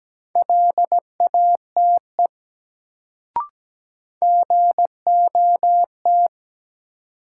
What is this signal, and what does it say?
Morse code, heard as a keyed tone, about 17 wpm, 700 Hz. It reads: LATE GOT
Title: Sentences from Top 300 Words - New Word Drill - 17wpm